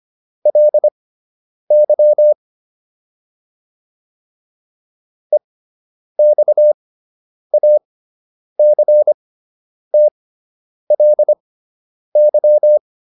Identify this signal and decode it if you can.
Morse code, heard as a keyed tone, about 25 wpm, spaced as for 9 wpm, 600 Hz. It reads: LY EXACTLY